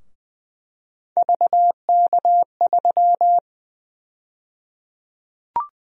Morse code, heard as a keyed tone, about 20 wpm, 700 Hz. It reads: VK3
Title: USA Call Sign Prefixes - 20wpm